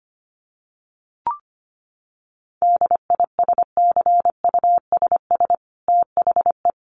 Morse code, heard as a keyed tone, about 25 wpm, 700 Hz. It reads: DISCUSS THE
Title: Sentences from Top 1500 Words 25wpm